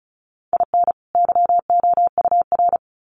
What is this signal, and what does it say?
Morse code, heard as a keyed tone, about 34 wpm, 700 Hz. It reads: IN YOUR
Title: Sets of 2 Words 35wpm